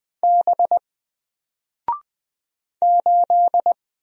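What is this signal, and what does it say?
Morse code, heard as a keyed tone, about 20 wpm, 700 Hz. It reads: B 8
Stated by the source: Single letter and number Morse drill at 20 wpm, from N4LCD.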